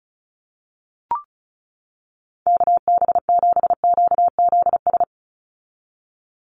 Morse code, heard as a keyed tone, about 34 wpm, 700 Hz. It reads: KB7QZS
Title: USA Call Signs 35wpm